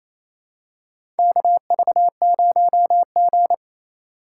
Morse code, heard as a keyed tone, about 28 wpm, 700 Hz. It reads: KV0G
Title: USA Call Signs - 4 Characters - 4x Character Spacing - 28wpm